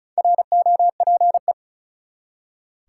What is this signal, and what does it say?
Morse code, heard as a keyed tone, about 34 wpm, 700 Hz. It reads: ROPE
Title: Four Letter Words - 35wpm